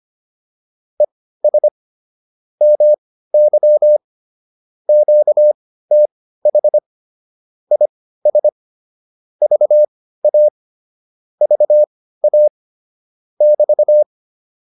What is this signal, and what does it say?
Morse code, heard as a keyed tone, about 25 wpm, spaced as for 15 wpm, 600 Hz. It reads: ES MY QTH IS VA VA <BT>